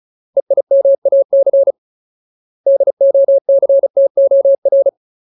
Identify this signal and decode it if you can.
Morse code, heard as a keyed tone, about 36 wpm, 550 Hz. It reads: EIMAC DOCTOR
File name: Ultimate Word List / 35WPM Ultimate